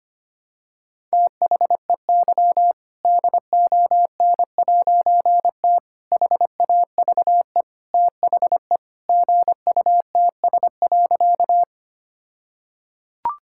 Morse code, heard as a keyed tone, about 25 wpm, 700 Hz. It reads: THEY DON'T HAVE THE GUTS.
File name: practice / 25wpm / All-Age Appropriate Jokes - 25wpm